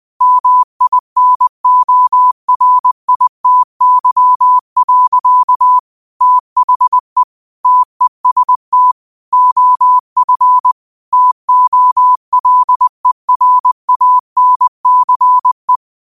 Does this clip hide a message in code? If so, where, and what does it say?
Yes: Morse code from 0.2 to 15.8 s, heard as a keyed tone, about 20 wpm, 1000 Hz. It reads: MINORITY. THE TEST OF TOLERANCE